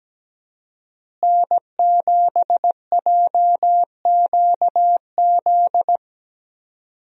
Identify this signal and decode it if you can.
Morse code, heard as a keyed tone, about 17 wpm, 700 Hz. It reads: N7JQZ
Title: USA Call Signs - 5 Characters - Repeated 3x - 17wpm